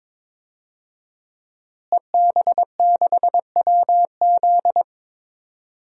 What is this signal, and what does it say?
Morse code, heard as a keyed tone, about 22 wpm, 700 Hz. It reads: EB6WZ
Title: DXCC-100 Call Signs - 5 Characters - Repeated 3x - 22wpm